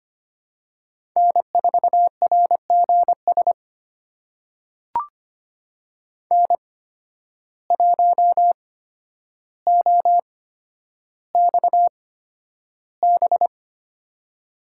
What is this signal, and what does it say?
Morse code, heard as a keyed tone, about 25 wpm, spaced as for 11 wpm, 700 Hz. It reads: N4RGS N1OXB
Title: USA Call Signs - 5 Characters - 8x Character Spacing - 25wpm